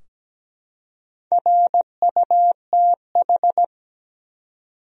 Morse code, heard as a keyed tone, about 17 wpm, 700 Hz. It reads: RUTH
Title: USA Common Names - Max 4 Letters - 17wpm